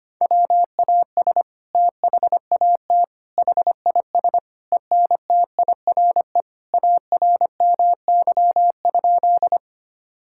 Morse code, heard as a keyed tone, about 25 wpm, 700 Hz. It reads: WAS THAT HIS ENTIRE ARMY?
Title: Sentences from Top 1000 Words - Review Part 5 - 25wpm